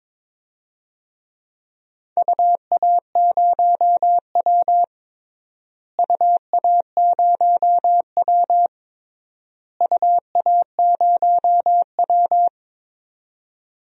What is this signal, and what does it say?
Morse code, heard as a keyed tone, about 22 wpm, 700 Hz. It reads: UA0W UA0W UA0W
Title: DXCC-100 Call Signs - 4 Characters - Repeated 3x - 22wpm